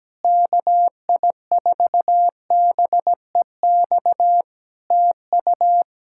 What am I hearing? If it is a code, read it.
Morse code, heard as a keyed tone, about 17 wpm, 700 Hz. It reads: KI4BEX TU
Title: K1USN SST QSO Practice 17wpm